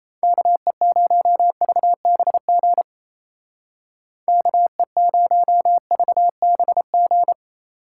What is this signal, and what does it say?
Morse code, heard as a keyed tone, about 34 wpm, 700 Hz. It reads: KE0VBG KE0VBG
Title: Speed Racing - USA Call Signs 28wpm